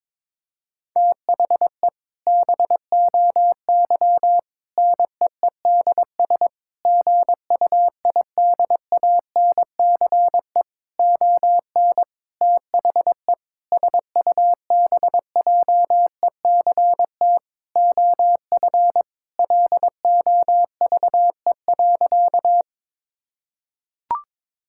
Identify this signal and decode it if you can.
Morse code, heard as a keyed tone, about 22 wpm, 700 Hz. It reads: THE BOY NEEDS GUIDANCE ON THE SUBJECT OF LOVE.
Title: Sentences from Top 1600 Words 22wpm